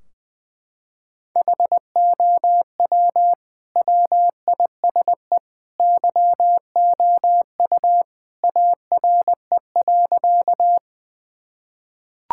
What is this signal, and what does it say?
Morse code, heard as a keyed tone, about 20 wpm, 700 Hz. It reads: HOW WISE YOU ARE.
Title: Sentences from Top 1600 Words 20wpm